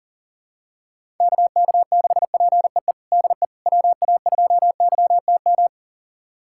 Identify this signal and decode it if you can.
Morse code, heard as a keyed tone, about 40 wpm, 700 Hz. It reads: KK6PEE DE WA2YTM
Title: USA Callsign DE Callsign - 6 Characters - 40wpm